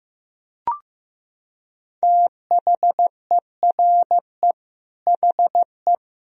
Morse code, heard as a keyed tone, about 15 wpm, 700 Hz. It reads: THERE HE
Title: Sentences from Top 100 Words and Abbreviations 15wpm